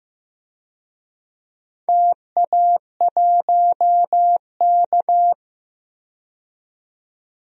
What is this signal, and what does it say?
Morse code, heard as a keyed tone, about 15 wpm, 700 Hz. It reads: TA1K
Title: DXCC-100 Call Signs - 4 Characters - 15wpm